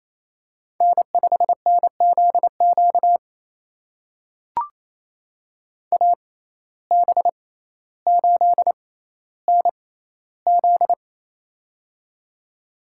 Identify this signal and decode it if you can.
Morse code, heard as a keyed tone, about 28 wpm, spaced as for 14 wpm, 700 Hz. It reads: N5NZQ AB8NZ
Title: USA Call Signs - 5 Characters - 6x Character Spacing - 28wpm